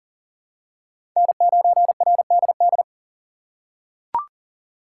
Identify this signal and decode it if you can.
Morse code, heard as a keyed tone, about 40 wpm, 700 Hz. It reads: N9RDD
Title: USA Call Signs - 5 Characters - 2x Character Spacing - 40wpm